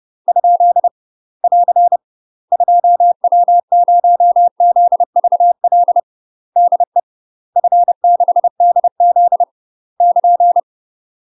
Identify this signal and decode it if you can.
Morse code, heard as a keyed tone, about 30 wpm, 700 Hz. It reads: ? <AR> 2W0ZVL DE F6DZ <KN>
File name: UK MASTER QSOs TRAINING / 30WPM / UK MASTER QSOs 0044